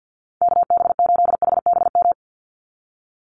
Morse code, heard as a keyed tone, about 50 wpm, 700 Hz. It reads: KB8HBM
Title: USA Call Signs 50wpm